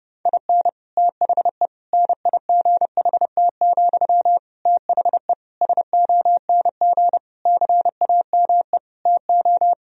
Morse code, heard as a keyed tone, about 31 wpm, 700 Hz. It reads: IN THE NIGHT, THE SONG CAME TO
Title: Sentences from Top 1000 Words - Review Part 2 - 30wpm